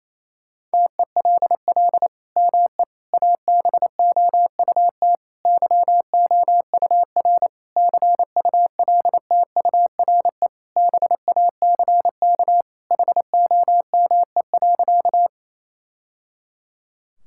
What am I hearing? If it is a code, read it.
Morse code, heard as a keyed tone, about 28 wpm, 700 Hz. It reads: TELL ME ABOUT YOUR CULTURE BACK HOME.
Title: Sentences from Top 1000 Words - Review Part 6 - 28wpm